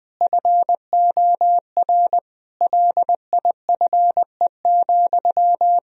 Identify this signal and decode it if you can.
Morse code, heard as a keyed tone, about 20 wpm, 700 Hz. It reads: FOR LIFE,
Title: Sentences from Top 1000 Words - Review Part 1 - 20wpm